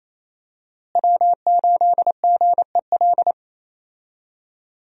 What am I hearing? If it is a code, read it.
Morse code, heard as a keyed tone, about 28 wpm, 700 Hz. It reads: W8GEL
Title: USA Call Signs - 5 Characters - Repeated 2x - 28wpm